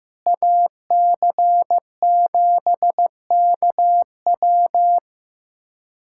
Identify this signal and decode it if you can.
Morse code, heard as a keyed tone, about 15 wpm, 700 Hz. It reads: AC7KW
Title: USA Callsign DE Callsign - 5 Characters - 15wpm